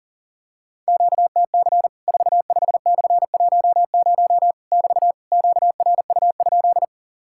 Morse code, heard as a keyed tone, about 40 wpm, 700 Hz. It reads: QTC 45/10 <BT> QRU?
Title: Worked All Europe DX Contest - QTC Exchange - 40wpm